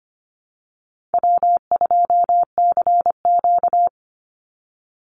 Morse code, heard as a keyed tone, about 25 wpm, 700 Hz. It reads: W2CQ
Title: Call Signs - 25wpm